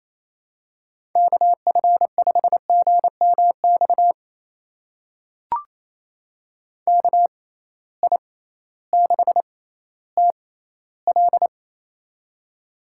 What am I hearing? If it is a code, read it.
Morse code, heard as a keyed tone, about 28 wpm, spaced as for 15 wpm, 700 Hz. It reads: KF5GMX KI6TL
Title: USA Call Signs - 6x Character Spacing - 28wpm